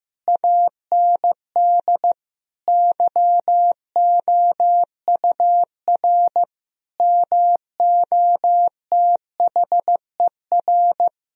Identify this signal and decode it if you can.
Morse code, heard as a keyed tone, about 16 wpm, 700 Hz. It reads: AND YOUR MOTHER
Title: Sentences from Top 1000 Words - Review Part 4 - 15wpm